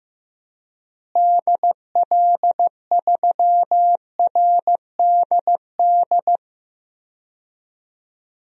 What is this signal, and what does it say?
Morse code, heard as a keyed tone, about 15 wpm, 700 Hz. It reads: DL3RDD
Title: DXCC-100 Call Signs - 6 Characters - Repeated 3x - 15wpm